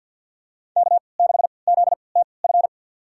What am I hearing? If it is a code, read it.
Morse code, heard as a keyed tone, about 51 wpm, spaced as for 28 wpm, 700 Hz. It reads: K6ZTF
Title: USA Call Signs - 5 Characters - 3x Character Spacing - 50wpm